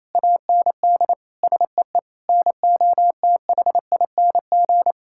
Morse code, heard as a keyed tone, about 27 wpm, 700 Hz. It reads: AND SEE NOTHING
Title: Sentences from Top 1000 Words - Review Part 6 - 28wpm